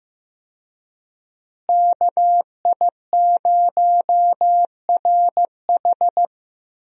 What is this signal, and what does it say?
Morse code, heard as a keyed tone, about 15 wpm, 700 Hz. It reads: KI0RH
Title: USA Call Signs - 5 Characters - Repeated 2x - 15wpm